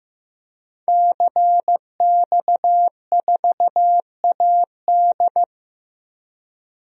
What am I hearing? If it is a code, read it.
Morse code, heard as a keyed tone, about 15 wpm, 700 Hz. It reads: CX4AD
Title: DXCC-100 Call Signs - 5 Characters - 15wpm